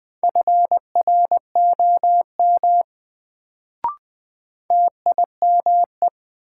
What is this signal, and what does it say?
Morse code, heard as a keyed tone, about 20 wpm, 700 Hz. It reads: FROM TIME
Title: Sets of 2 Words - New Word Drill - 20wpm